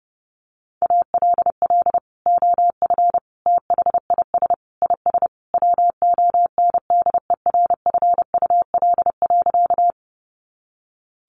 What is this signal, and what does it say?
Morse code, heard as a keyed tone, about 30 wpm, 700 Hz. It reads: ALL OF THIS IS WONDERFUL.
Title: Sentences from Top 1100 Words 30wpm